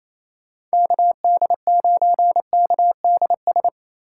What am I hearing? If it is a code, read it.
Morse code, heard as a keyed tone, about 28 wpm, 700 Hz. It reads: KD9KDS
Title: USA Call Signs - 6 Characters - 2x Character Spacing - 28wpm